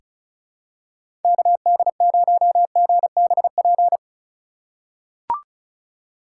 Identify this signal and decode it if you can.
Morse code, heard as a keyed tone, about 36 wpm, 700 Hz. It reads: KD0GBP
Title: USA Call Signs - 6x Character Spacing - 35wpm